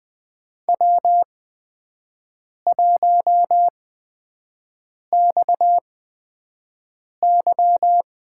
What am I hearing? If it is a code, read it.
Morse code, heard as a keyed tone, about 20 wpm, spaced as for 5 wpm, 700 Hz. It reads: W1XY
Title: USA Call Signs - 5 Characters - 8x Character Spacing - 20wpm